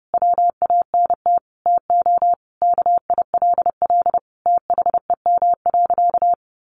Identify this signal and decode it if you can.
Morse code, heard as a keyed tone, about 30 wpm, 700 Hz. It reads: WANT TO KILL THEM.
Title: Easy Phrases 30wpm